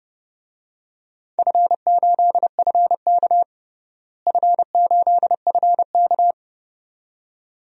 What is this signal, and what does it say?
Morse code, heard as a keyed tone, about 30 wpm, 700 Hz. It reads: F8FK F8FK
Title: DXCC-100 Call Signs - 4 Characters - Repeated 2x - 30wpm